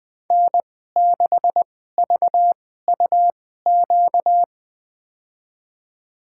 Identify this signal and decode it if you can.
Morse code, heard as a keyed tone, about 20 wpm, spaced as for 14 wpm, 700 Hz. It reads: N6VUQ
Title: USA Call Signs - 5 Characters - 2x Character Spacing - 20wpm